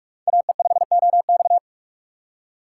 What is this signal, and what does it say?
Morse code, heard as a keyed tone, about 46 wpm, 700 Hz. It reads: AE5OX
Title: USA Call Signs - 5 Characters - 3x Character Spacing - 45wpm